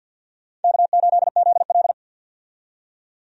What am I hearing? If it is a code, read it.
Morse code, heard as a keyed tone, about 52 wpm, 700 Hz. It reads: K8ZL
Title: USA Call Signs - 4 Characters - 50wpm